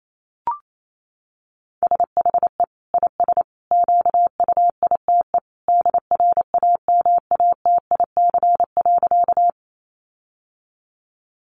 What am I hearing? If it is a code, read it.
Morse code, heard as a keyed tone, about 28 wpm, 700 Hz. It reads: SHE IS QUITE DRAMATIC.
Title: Sentences from Top 1300 Words 28wpm